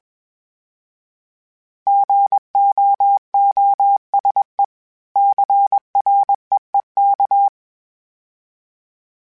Morse code, heard as a keyed tone, about 21 wpm, 800 Hz. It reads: GOOSE CREEK
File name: fromSMs / FromSMs mp3s / Cities@20